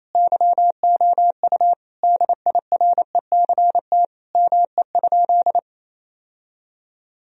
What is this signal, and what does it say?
Morse code, heard as a keyed tone, about 27 wpm, 700 Hz. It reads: YOU DIRECT ME?
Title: Sentences from Top 1000 Words - Review Part 2 - 28wpm